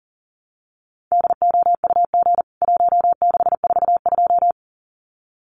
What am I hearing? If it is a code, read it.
Morse code, heard as a keyed tone, about 40 wpm, 700 Hz. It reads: DOUG 1642
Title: CWT Exchange 40wpm